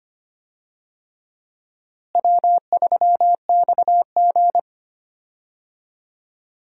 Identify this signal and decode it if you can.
Morse code, heard as a keyed tone, about 25 wpm, 700 Hz. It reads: W3XG
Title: USA Call Signs - 4 Characters - Repeated 3x - 25wpm